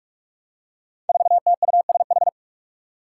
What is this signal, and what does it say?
Morse code, heard as a keyed tone, about 46 wpm, 700 Hz. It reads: 4TUSH
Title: Five Challenging Letter-Number Combinations 45wpm